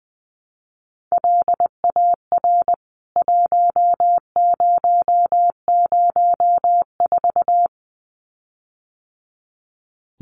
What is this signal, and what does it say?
Morse code, heard as a keyed tone, about 20 wpm, 700 Hz. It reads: LAR 1004
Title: CWT Exchange 20wpm